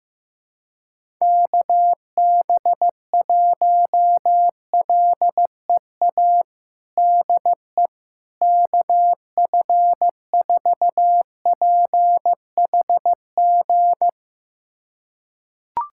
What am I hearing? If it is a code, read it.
Morse code, heard as a keyed tone, about 15 wpm, 700 Hz. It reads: KB1LEA DE KF4PHG
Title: USA Callsign DE Callsign 15wpm